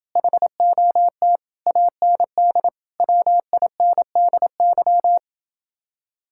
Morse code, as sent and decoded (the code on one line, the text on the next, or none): .... --- - / .- -. -.. / .-- .. -. -.. -.--
HOT AND WINDY